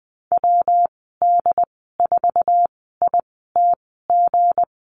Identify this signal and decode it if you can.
Morse code, heard as a keyed tone, about 20 wpm, spaced as for 15 wpm, 700 Hz. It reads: WD4ITG
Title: 167 Lesson 27g - Number 7 - Call Signs 20wpm